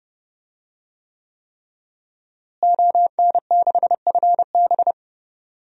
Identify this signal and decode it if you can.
Morse code, heard as a keyed tone, about 30 wpm, 700 Hz. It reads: ON6FB
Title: DXCC-100 Call Signs - 5 Characters - Repeated 3x - 30wpm